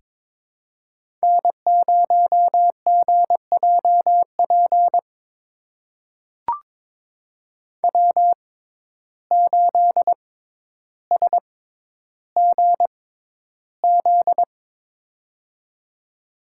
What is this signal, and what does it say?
Morse code, heard as a keyed tone, about 22 wpm, spaced as for 11 wpm, 700 Hz. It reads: N0GJP W8SGZ
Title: USA Call Signs - 5 Characters - 6x Character Spacing - 22wpm